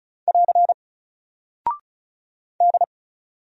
Morse code, heard as a keyed tone, about 36 wpm, 700 Hz. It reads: <AR> D
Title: Single Character (Letter:Number:Punctuation:Prosign) Alt 35wpm